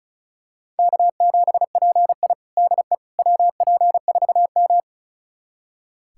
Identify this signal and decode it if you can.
Morse code, heard as a keyed tone, about 36 wpm, 700 Hz. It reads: K7PI DE WP4M
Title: USA Callsign DE Callsign - 4 Characters - 35wpm